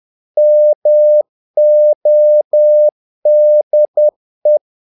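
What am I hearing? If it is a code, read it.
Morse code, heard as a keyed tone, about 10 wpm, 600 Hz. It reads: MODE
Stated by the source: Morse practice of short Danish words at 10 wpm, from CW med Gnister.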